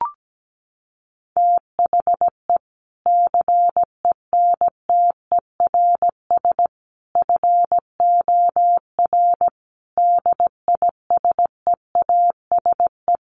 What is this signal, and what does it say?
Morse code, heard as a keyed tone, about 17 wpm, 700 Hz. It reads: THE CENTERS FOR DISEASE
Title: Sets of 5 Words Encore2 17wpm